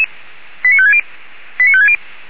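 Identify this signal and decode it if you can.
Five-tone selective call (ZVEI), heard as quick runs of five tones, tones of about 70 ms. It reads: 89560 89560
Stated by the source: ZVEI five-tone selective call received on SDR.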